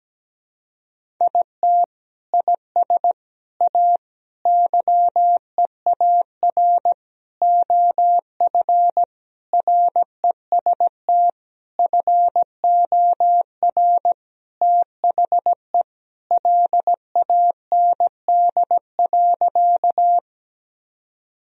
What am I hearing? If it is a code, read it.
Morse code, heard as a keyed tone, about 17 wpm, 700 Hz. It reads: IT IS A YEAR OF REST FOR THE LAND.
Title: Sentences from Top 1000 Words - Review Part 1 - 17wpm